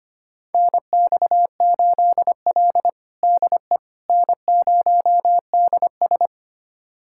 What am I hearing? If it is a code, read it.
Morse code, heard as a keyed tone, about 25 wpm, 700 Hz. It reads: NX8L DE N0DS
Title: USA Callsign DE Callsign - 4 Characters - 25wpm